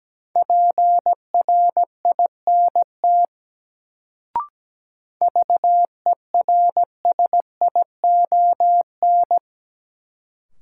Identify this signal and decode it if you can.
Morse code, heard as a keyed tone, about 17 wpm, 700 Hz. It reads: PRINT VERSION
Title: Sentences from Top 1400 Words - New Word Drill Alt - 17wpm